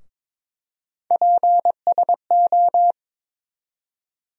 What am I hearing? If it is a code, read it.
Morse code, heard as a keyed tone, about 22 wpm, 700 Hz. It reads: PSO
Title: USA Call Sign Postfixes - 3 Letters 22wpm